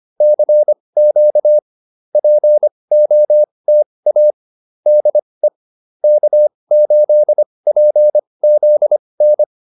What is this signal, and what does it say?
Morse code, heard as a keyed tone, about 25 wpm, spaced as for 20 wpm, 600 Hz. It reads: CQ POTA DE K8PZN